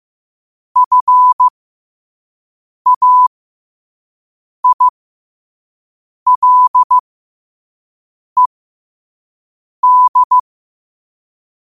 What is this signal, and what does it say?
Morse code, heard as a keyed tone, about 15 wpm, spaced as for 5 wpm, 1000 Hz. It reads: FAILED
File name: qotdfiles / 5wpm